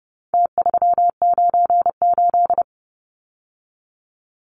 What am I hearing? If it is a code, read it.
Morse code, heard as a keyed tone, about 30 wpm, 700 Hz. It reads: T398
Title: Numbers - Contest Style - 4-Digit 30wpm